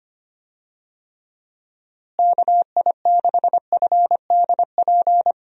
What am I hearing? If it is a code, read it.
Morse code, heard as a keyed tone, about 25 wpm, 700 Hz. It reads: KI6FDP
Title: USA Call Signs - 6 Characters - Repeated 3x - 25wpm